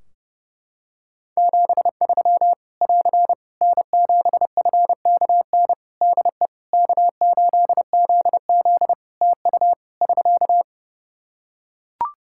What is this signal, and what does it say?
Morse code, heard as a keyed tone, about 29 wpm, 700 Hz. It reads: 73 <AR> N7FKN DE K8ZZ TU <SK>
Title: Practice QSO - Exchange 4 - Ending - 30wpm